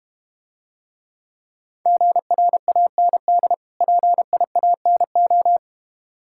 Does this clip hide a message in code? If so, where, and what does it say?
Yes: Morse code from 1.9 to 5.6 s, heard as a keyed tone, about 31 wpm, 700 Hz. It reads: GRAND PIANO